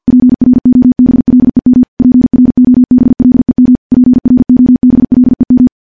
Morse code, heard as a keyed tone, about 46 wpm, 250 Hz. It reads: PROBLEM PROBLEM PROBLEM
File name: morse / morse-training-50WPM